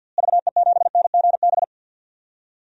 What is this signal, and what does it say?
Morse code, heard as a keyed tone, about 47 wpm, 700 Hz. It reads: VE7NGB
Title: DXCC-100 Call Signs - 6 Characters - Repeated 2x - 50wpm